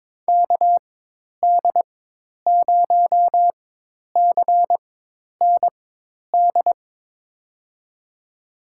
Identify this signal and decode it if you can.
Morse code, heard as a keyed tone, about 23 wpm, spaced as for 10 wpm, 700 Hz. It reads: KD0CND
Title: USA Call Signs - 4x Character Spacing - 22wpm